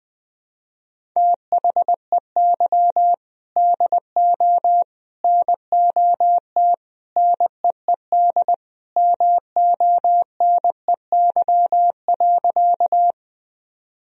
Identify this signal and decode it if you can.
Morse code, heard as a keyed tone, about 20 wpm, 700 Hz. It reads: THEY DO NOT NEED MONEY.
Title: Sentences from Top 1000 Words - Review Part 1 - 20wpm